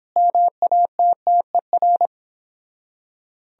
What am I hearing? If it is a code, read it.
Morse code, heard as a keyed tone, about 26 wpm, 700 Hz. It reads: MATTER
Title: Speed-Racing - Sets of 5 Words 22wpm